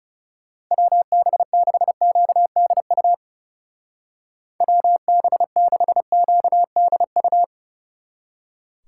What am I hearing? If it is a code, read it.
Morse code, heard as a keyed tone, about 40 wpm, 700 Hz. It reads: WB6QDU WB6QDU